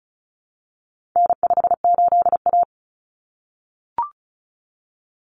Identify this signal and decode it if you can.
Morse code, heard as a keyed tone, about 34 wpm, 700 Hz. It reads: N58A
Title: Common Cut Numbers - 4-Digit 35wpm